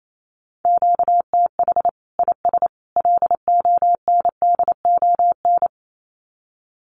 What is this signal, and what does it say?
Morse code, heard as a keyed tone, about 28 wpm, 700 Hz. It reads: QTH IS LONDON
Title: Common QSO Phrases 28wpm